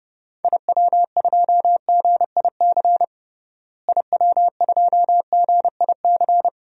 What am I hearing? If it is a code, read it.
Morse code, heard as a keyed tone, about 32 wpm, 700 Hz. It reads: IW2GIC IW2GIC
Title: DXCC-100 Call Signs - 6 Characters - Repeated 2x - 30wpm